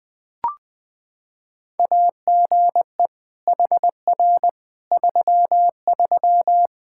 Morse code, heard as a keyed tone, about 20 wpm, 700 Hz. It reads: AGE HR 33
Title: K1USN SST QSO Practice 20wpm